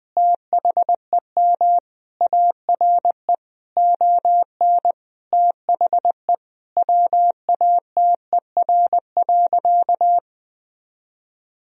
Morse code, heard as a keyed tone, about 20 wpm, 700 Hz. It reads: THEM ARE ON THE WATER.
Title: Sentences from Top 1000 Words - Review Part 1 - 20wpm